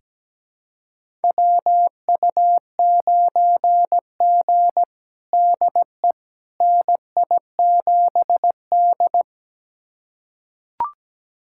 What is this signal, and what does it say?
Morse code, heard as a keyed tone, about 17 wpm, 700 Hz. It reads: WU9G DE NI7D
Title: USA Callsign DE Callsign - 4 Characters - 17wpm